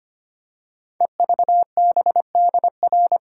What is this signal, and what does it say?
Morse code, heard as a keyed tone, about 25 wpm, 700 Hz. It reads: EVBDR
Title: Code Groups - 5 Letter:Numbers - 25wpm